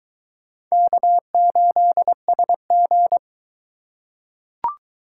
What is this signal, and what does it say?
Morse code, heard as a keyed tone, about 22 wpm, 700 Hz. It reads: K8SG